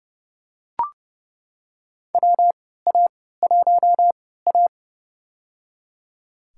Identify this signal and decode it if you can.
Morse code, heard as a keyed tone, about 30 wpm, spaced as for 17 wpm, 700 Hz. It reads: WA1A
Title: USA Call Signs - 4 Characters - 3x Character Spacing - 30wpm